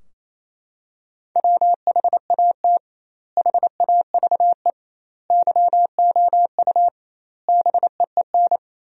Morse code, heard as a keyed tone, about 28 wpm, 700 Hz. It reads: WHAT HAVE YOU BEEN
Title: 2x Word-Spacing - Sentences from Top 200 Words - 28wpm